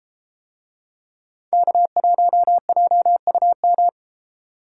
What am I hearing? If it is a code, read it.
Morse code, heard as a keyed tone, about 33 wpm, 700 Hz. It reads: K1JUM